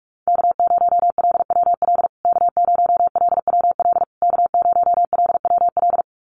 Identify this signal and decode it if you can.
Morse code, heard as a keyed tone, about 44 wpm, 700 Hz. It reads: K0LWL K0LWL K0LWL
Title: USA Call Signs Repeated 3x 45wpm